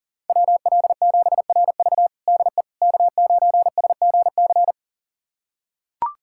Morse code, heard as a keyed tone, about 40 wpm, 700 Hz. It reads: WL7RV DE K9SGC
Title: USA Callsign DE Callsign - 5 Characters - 40wpm